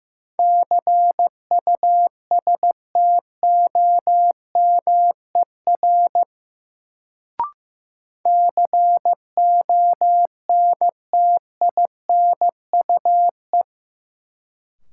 Morse code, heard as a keyed tone, about 15 wpm, 700 Hz. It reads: CUSTOMER CONTINUE
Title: Sentences from Top 1100 Words - New Word Drill Alt - 15wpm